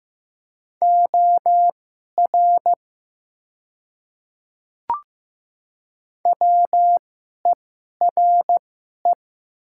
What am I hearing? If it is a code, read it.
Morse code, heard as a keyed tone, about 15 wpm, spaced as for 11 wpm, 700 Hz. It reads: OR WERE
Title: 2x Character Spacing - Top 100 Words - 15wpm